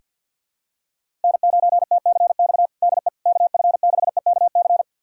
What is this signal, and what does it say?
Morse code, heard as a keyed tone, about 46 wpm, 700 Hz. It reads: N9TCX DE KF6EKC
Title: USA Callsign DE Callsign 50wpm